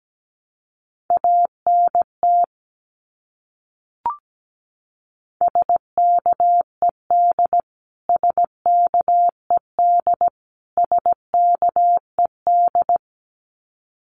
Morse code, heard as a keyed tone, about 17 wpm, 700 Hz. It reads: ANT SKED SKED SKED
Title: Abbreviations Repeated 3x - 17wpm